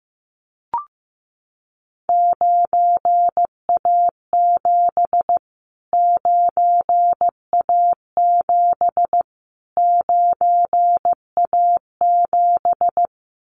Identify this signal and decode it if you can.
Morse code, heard as a keyed tone, about 15 wpm, 700 Hz. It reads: 9A7 9A7 9A7
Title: Callsign Prefixes for Top 25 DXCC - Repeated 3x - 15wpm